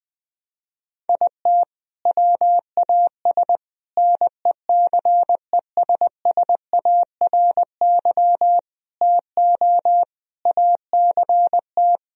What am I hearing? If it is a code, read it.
Morse code, heard as a keyed tone, about 20 wpm, 700 Hz. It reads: IT WAS NECESSARY TO ACT
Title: Sentences from Top 1000 Words - Review Part 1 - 20wpm